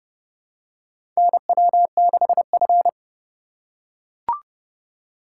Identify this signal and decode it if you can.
Morse code, heard as a keyed tone, about 30 wpm, 700 Hz. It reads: NW6F
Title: USA Call Signs - 4 Characters - 4x Character Spacing - 30wpm